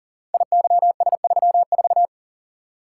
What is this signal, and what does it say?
Morse code, heard as a keyed tone, about 38 wpm, 700 Hz. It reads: IYS34
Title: Code Groups - 5 Letter:Numbers - 40wpm